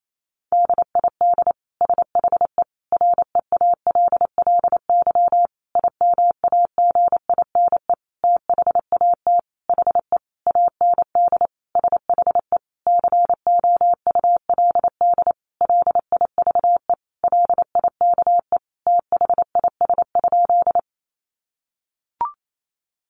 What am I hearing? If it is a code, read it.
Morse code, heard as a keyed tone, about 28 wpm, 700 Hz. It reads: DID SHE REALLY IMAGINE THAT HE AND SHE COULD LIVE LIKE THIS?
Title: Sentences from Top 1400 Words 28wpm